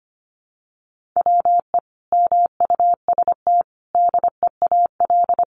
Morse code, heard as a keyed tone, about 25 wpm, 700 Hz. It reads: WE MUST DEAL
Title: Sentences from Top 1400 Words 25wpm